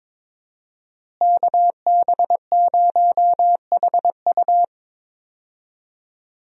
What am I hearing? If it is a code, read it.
Morse code, heard as a keyed tone, about 21 wpm, 700 Hz. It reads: KB0HU
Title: USA Call Signs - 5 Characters - Repeated 2x - 22wpm